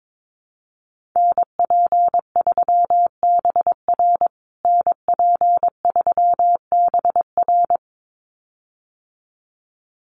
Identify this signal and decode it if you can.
Morse code, heard as a keyed tone, about 22 wpm, 700 Hz. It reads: NP3BR NP3BR
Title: USA Call Signs Repeated 2x 22wpm